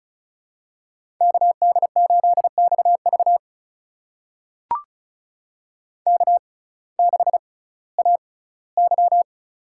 Morse code, heard as a keyed tone, about 36 wpm, spaced as for 19 wpm, 700 Hz. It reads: KD8XV K6AY
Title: USA Call Signs - 6x Character Spacing - 35wpm